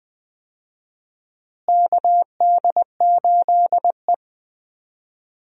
Morse code, heard as a keyed tone, about 21 wpm, 700 Hz. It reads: KD8E
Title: USA Call Signs - 4 Characters - Repeated 2x - 20wpm